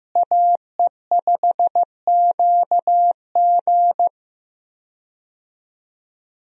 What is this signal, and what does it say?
Morse code, heard as a keyed tone, about 16 wpm, 700 Hz. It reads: AE5QG
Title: USA Call Signs - 5 Characters - Repeated 3x - 15wpm